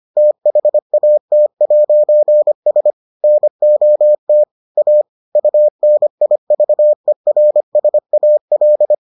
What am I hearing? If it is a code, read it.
Morse code, heard as a keyed tone, about 25 wpm, 600 Hz. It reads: THAT'S NOT A UNIVERSAL